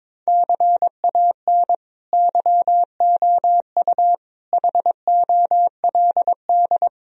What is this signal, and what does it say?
Morse code, heard as a keyed tone, about 22 wpm, 700 Hz. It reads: CAN YOU HOLD
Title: Sentences from Top 1000 Words - Review Part 2 - 22wpm